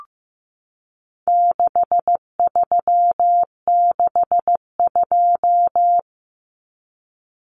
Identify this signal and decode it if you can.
Morse code, heard as a keyed tone, about 15 wpm, 700 Hz. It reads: 6362
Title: Numbers - 4-Digit 15wpm